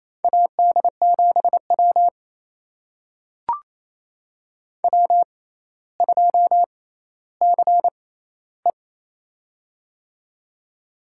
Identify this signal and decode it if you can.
Morse code, heard as a keyed tone, about 28 wpm, spaced as for 14 wpm, 700 Hz. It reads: AD7W W2CE
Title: USA Call Signs - 4 Characters - 6x Character Spacing - 28wpm